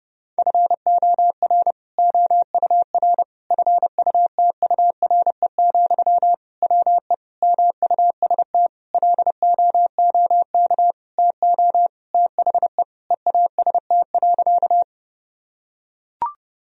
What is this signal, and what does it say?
Morse code, heard as a keyed tone, about 29 wpm, 700 Hz. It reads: FOR OUR FUTURE, WE MUST LOOK TO THE EAST.
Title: Sentences from Top 1000 Words - Review Part 3 - 30wpm